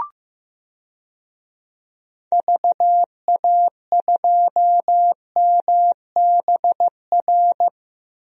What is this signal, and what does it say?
Morse code, heard as a keyed tone, about 16 wpm, 700 Hz. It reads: VA2MBR
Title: DXCC-100 Call Signs - 6 Characters - Repeated 3x - 15wpm